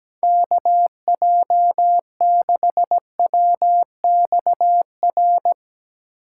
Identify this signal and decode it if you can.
Morse code, heard as a keyed tone, about 18 wpm, 700 Hz. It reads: KJ6WXR
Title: Speed Racing - USA 2x3 Call Signs 17wpm